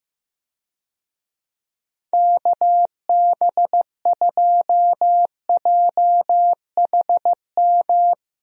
Morse code, heard as a keyed tone, about 15 wpm, 700 Hz. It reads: KB2JHM
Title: USA Call Signs - 6 Characters - 15wpm